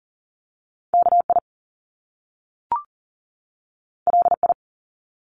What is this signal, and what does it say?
Morse code, heard as a keyed tone, about 40 wpm, 700 Hz. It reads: KI LI
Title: Top Two Letter Combinations in Words 40wpm